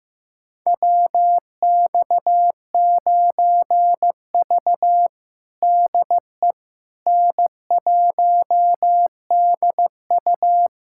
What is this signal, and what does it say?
Morse code, heard as a keyed tone, about 15 wpm, 700 Hz. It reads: WX9V DE N1DU
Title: USA Callsign DE Callsign - 4 Characters - 15wpm